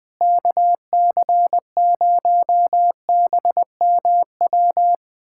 Morse code, heard as a keyed tone, about 21 wpm, 700 Hz. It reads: KC0BMW
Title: USA Callsign DE Callsign - 6 Characters - 20wpm